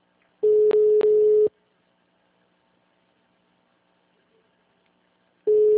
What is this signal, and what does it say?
Ringback tone (Europe, 425 Hz).